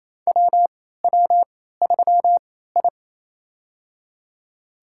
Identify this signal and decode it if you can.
Morse code, heard as a keyed tone, about 28 wpm, spaced as for 16 wpm, 700 Hz. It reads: WW3I
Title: USA Call Signs - 3x Character Spacing - 28wpm